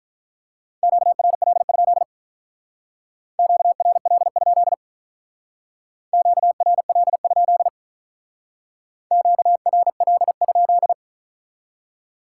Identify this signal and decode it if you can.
Morse code, heard as a keyed tone, about 46 wpm, 700 Hz. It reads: QRL? QRL? QRL? QRL?